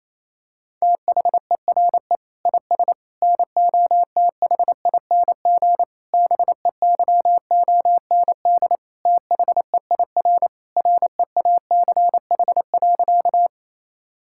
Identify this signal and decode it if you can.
Morse code, heard as a keyed tone, about 28 wpm, 700 Hz. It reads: THERE IS NOTHING BEYOND THEIR REACH.
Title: Sentences from Top 1000 Words - Review Part 6 - 28wpm